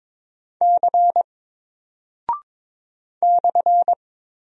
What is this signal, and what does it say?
Morse code, heard as a keyed tone, about 22 wpm, 700 Hz. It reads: C /
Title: Single Character (Letter:Number:Punctuation:Prosign) Alt 22wpm